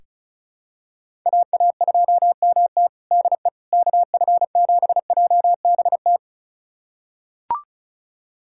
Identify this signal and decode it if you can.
Morse code, heard as a keyed tone, about 36 wpm, 700 Hz. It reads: AA2MT DE KF7JBT
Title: USA Callsign DE Callsign 35wpm